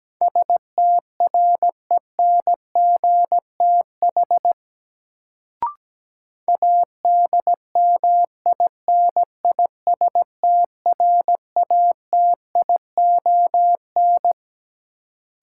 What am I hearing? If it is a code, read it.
Morse code, heard as a keyed tone, about 17 wpm, 700 Hz. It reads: STRENGTH ADMINISTRATION
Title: Sentences from Top 1300 Words - New Word Drill Alt - 17wpm